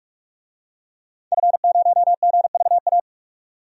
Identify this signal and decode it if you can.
Morse code, heard as a keyed tone, about 46 wpm, 700 Hz. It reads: F0GVA
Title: DXCC-100 Call Signs - 5 Characters - 45wpm